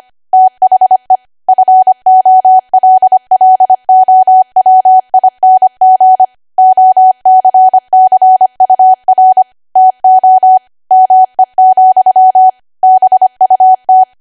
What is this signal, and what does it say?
Morse code, heard as a keyed tone, about 25 wpm, 750 Hz. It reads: THE FOLLOWING OCCUR TO ME, BUT
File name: Archive / 25 WPM / 25WPM